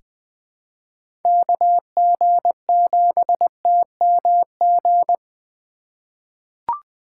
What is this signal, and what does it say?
Morse code, heard as a keyed tone, about 20 wpm, 700 Hz. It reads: KG7TMG
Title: USA Call Signs - 6 Characters - 4x Character Spacing - 20wpm